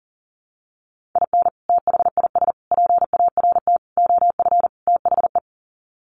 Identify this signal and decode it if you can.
Morse code, heard as a keyed tone, about 40 wpm, 700 Hz. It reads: IN THIS PART OF THE
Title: Sets of 5 Words 40wpm